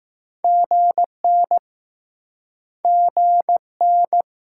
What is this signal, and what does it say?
Morse code, heard as a keyed tone, about 20 wpm, 700 Hz. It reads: GN GN